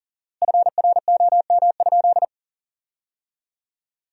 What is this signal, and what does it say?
Morse code, heard as a keyed tone, about 40 wpm, 700 Hz. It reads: FROM?
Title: 4x Word-Spacing - Sentences from Top 200 Words - 40wpm